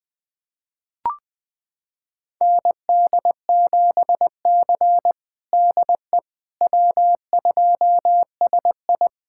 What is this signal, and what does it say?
Morse code, heard as a keyed tone, about 20 wpm, 700 Hz. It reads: ND7C DE W2SI
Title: USA Callsign DE Callsign - 4 Characters - 20wpm